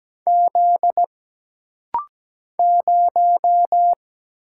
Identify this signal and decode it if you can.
Morse code, heard as a keyed tone, about 17 wpm, 700 Hz. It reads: Z 0
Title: Single Letter-Number - Rapid-Fire 17wpm